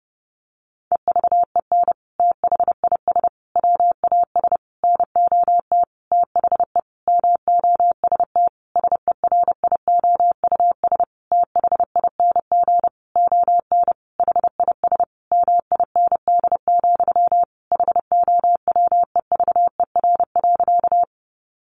Morse code, heard as a keyed tone, about 30 wpm, 700 Hz. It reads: EVEN THIS WAS NOT THE MOST SERIOUS THING ON HIS MIND, HOWEVER.
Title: Sentences from Top 1100 Words 30wpm